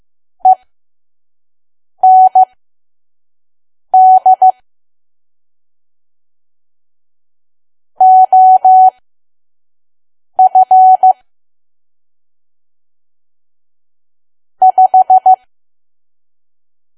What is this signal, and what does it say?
Morse code, heard as a keyed tone, about 15 wpm, spaced as for 5 wpm, 750 Hz. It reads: END OF 5
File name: Archive / 5 WPM files / PM